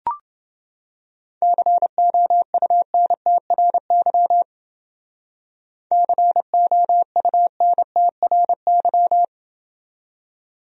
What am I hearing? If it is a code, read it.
Morse code, heard as a keyed tone, about 28 wpm, 700 Hz. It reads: COUNTRY COUNTRY